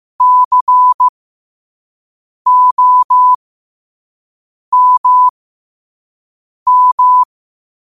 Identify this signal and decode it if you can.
Morse code, heard as a keyed tone, about 15 wpm, spaced as for 5 wpm, 1000 Hz. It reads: COMM
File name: qotdfiles / 5wpm